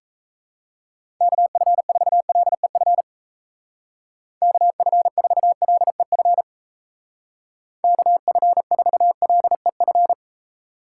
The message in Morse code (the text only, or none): KF4LEF KF4LEF KF4LEF